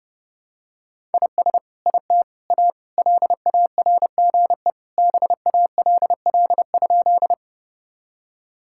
Morse code, heard as a keyed tone, about 32 wpm, 700 Hz. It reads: IS IT A LARGE BALL?
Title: Sentences from Top 1000 Words - Review Part 6 - 30wpm